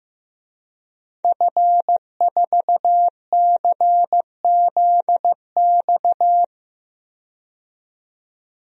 Morse code, heard as a keyed tone, about 15 wpm, 700 Hz. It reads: F4CZX
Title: DXCC-100 Call Signs - 5 Characters - 15wpm